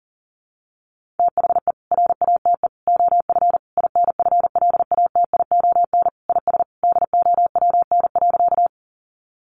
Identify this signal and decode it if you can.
Morse code, heard as a keyed tone, about 40 wpm, 700 Hz. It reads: THE RATE OF INFLATION IS DOWN.
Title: Sentences from Top 1100 Words 40wpm